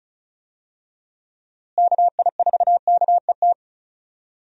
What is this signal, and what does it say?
Morse code, heard as a keyed tone, about 34 wpm, 700 Hz. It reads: KI4KET
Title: USA Call Signs - 6 Characters - Repeated 3x - 35wpm